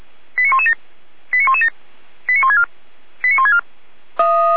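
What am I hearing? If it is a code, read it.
Five-tone selective call (ZVEI), heard as quick runs of five tones, tones of about 70 ms. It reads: 89117 89117 89164 89164
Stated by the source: ZVEI five-tone selective call 89504, received by SDR.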